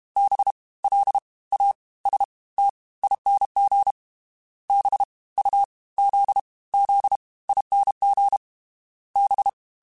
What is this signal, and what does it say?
Morse code, heard as a keyed tone, about 32 wpm, 800 Hz. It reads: B L A S T ING B U Z Z ING B
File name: Special INGs / Special INGs-mp3s / Fast ing @30